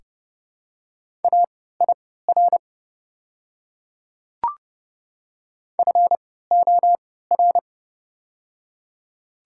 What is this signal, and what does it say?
Morse code, heard as a keyed tone, about 31 wpm, spaced as for 17 wpm, 700 Hz. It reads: AIR FOR
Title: 3x Character Spacing - Top 200 Words - 30wpm